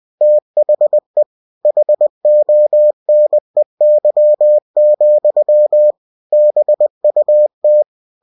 Morse code, heard as a keyed tone, about 20 wpm, 600 Hz. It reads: THE HONEY, BUT
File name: cw-books / winnie-the-pooh / winnie-20wpm-page-0007